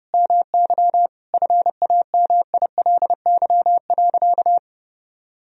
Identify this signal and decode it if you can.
Morse code, heard as a keyed tone, about 32 wpm, 700 Hz. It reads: MY FAMILY.
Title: Sentences from Top 1000 Words - Review Part 2 - 30wpm